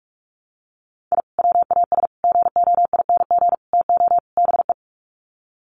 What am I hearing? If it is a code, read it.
Morse code, heard as a keyed tone, about 44 wpm, 700 Hz. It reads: I WAS GOING TO BE
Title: Sets of 5 Words 45wpm